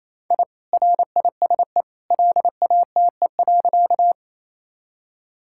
Morse code, heard as a keyed tone, about 28 wpm, 700 Hz. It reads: I RISE LATE.
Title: Sentences from Top 1000 Words - Review Part 6 - 28wpm